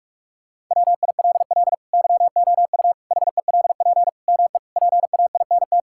Morse code, heard as a keyed tone, about 46 wpm, 700 Hz. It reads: WILL YOU HELP ME PAINT
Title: Sentences from Top 1000 Words - Review Part 6 - 45wpm